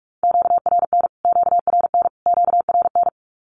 Morse code, heard as a keyed tone, about 44 wpm, 700 Hz. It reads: QRN QRN QRN
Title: Q-Codes Repeated 3x 45wpm